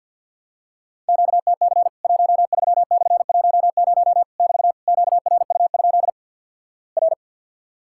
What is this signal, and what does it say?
Morse code, heard as a keyed tone, about 53 wpm, 700 Hz. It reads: QTC 13/10 <BT> QRU? R
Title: Worked All Europe DX Contest - QTC Exchange with Repeats - 50wpm